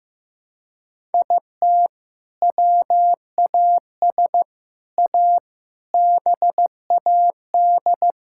Morse code, heard as a keyed tone, about 14 wpm, 700 Hz. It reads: IT WAS A BAD
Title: Sentences from Top 1000 Words - Review Part 6 - 15wpm